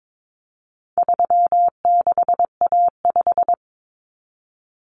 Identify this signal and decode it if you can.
Morse code, heard as a keyed tone, about 22 wpm, 700 Hz. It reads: 36A5
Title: Common Cut Numbers - 4-Digit 22wpm